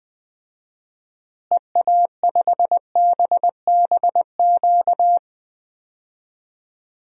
Morse code, heard as a keyed tone, about 21 wpm, 700 Hz. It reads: EA5BBQ
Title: DXCC-100 Call Signs - 6 Characters - Repeated 2x - 20wpm